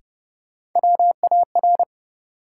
Morse code, heard as a keyed tone, about 30 wpm, 700 Hz. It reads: WAR
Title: Sets of 5 Words - New Word Drill - 30wpm